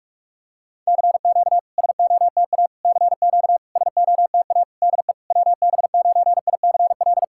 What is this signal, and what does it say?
Morse code, heard as a keyed tone, about 46 wpm, 700 Hz. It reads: CQ SOTA CQ SOTA DE WB9ICL
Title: SOTA QSO Practice 45wpm